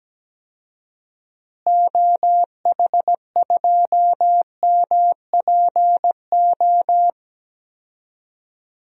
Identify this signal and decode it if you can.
Morse code, heard as a keyed tone, about 17 wpm, 700 Hz. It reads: OH2MPO
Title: DXCC-100 Call Signs - 6 Characters - Repeated 3x - 17wpm